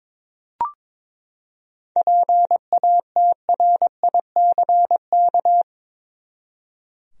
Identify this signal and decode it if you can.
Morse code, heard as a keyed tone, about 22 wpm, 700 Hz. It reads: PATRICK
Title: USA Common Names - 22wpm